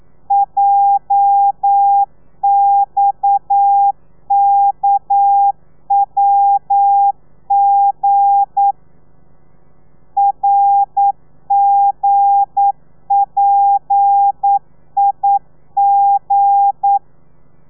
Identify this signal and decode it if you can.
Morse code, heard as a keyed tone, about 9 wpm, 800 Hz. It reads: JXKWG RGPIG
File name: telegrafia / i2tfj / Lez8 40